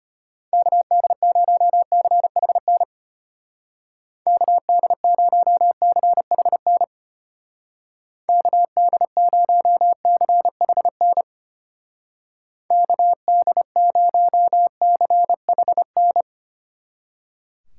Morse code, heard as a keyed tone, about 31 wpm, 700 Hz. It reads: KD0CHN KD0CHN KD0CHN KD0CHN